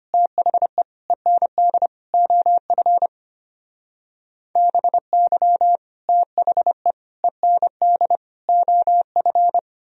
Morse code, heard as a keyed tone, about 31 wpm, 700 Hz. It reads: THE END OF BY THE END OF
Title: Speed-Racing - Sets of 4 Words 25wpm